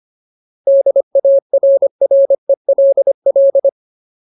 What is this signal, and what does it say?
Morse code, heard as a keyed tone, about 25 wpm, 550 Hz. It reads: DARRELL
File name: Ultimate Word List / 25WPM Ultimate